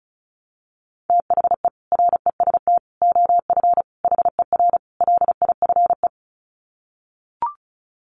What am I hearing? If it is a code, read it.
Morse code, heard as a keyed tone, about 34 wpm, 700 Hz. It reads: THE REST OF HER LIFE
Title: Sets of 5 Words Encore 35wpm